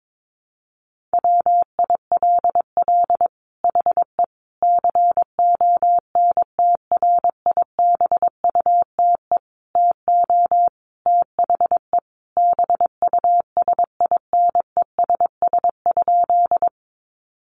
Morse code, heard as a keyed tone, about 22 wpm, 700 Hz. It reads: WILL HE CONTRIBUTE TO THE BUSINESS?
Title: Sentences from Top 1400 Words 22wpm